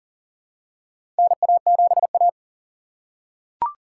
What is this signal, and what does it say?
Morse code, heard as a keyed tone, about 39 wpm, 700 Hz. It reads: NA7A
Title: USA Call Signs - 4 Characters - 2x Character Spacing - 40wpm